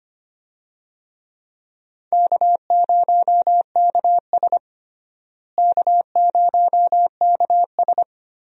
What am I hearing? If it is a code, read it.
Morse code, heard as a keyed tone, about 25 wpm, 700 Hz. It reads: K0KS K0KS
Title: USA Call Signs - 4 Characters - Repeated 2x - 25wpm